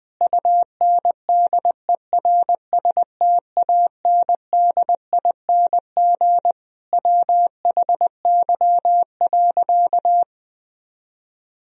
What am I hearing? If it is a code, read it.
Morse code, heard as a keyed tone, about 21 wpm, 700 Hz. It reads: UNDERSTANDING WHY.
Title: Sentences from Top 1000 Words - Review Part 3 - 20wpm